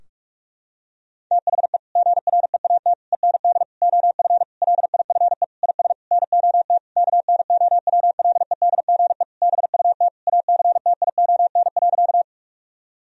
Morse code, heard as a keyed tone, about 46 wpm, 700 Hz. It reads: THE GREAT END OF LIFE IS NOT KNOWLEDGE BUT ACTION.
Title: Sentences from Top 1000 Words - Review Part 4 - 45wpm